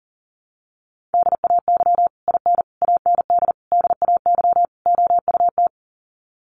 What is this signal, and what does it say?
Morse code, heard as a keyed tone, about 40 wpm, 700 Hz. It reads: DAY IN AND DAY OUT
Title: Sets of 5 Words Encore2 40wpm